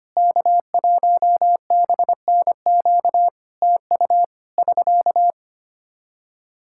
Morse code, heard as a keyed tone, about 25 wpm, 700 Hz. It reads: K1BNQ TU <SK>
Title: K1USN SST QSO Practice 25wpm